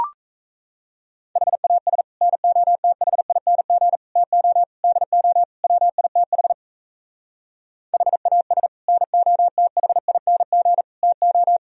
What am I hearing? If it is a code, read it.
Morse code, heard as a keyed tone, about 38 wpm, 700 Hz. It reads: HAS NOTHING TO DO WITH HAS NOTHING TO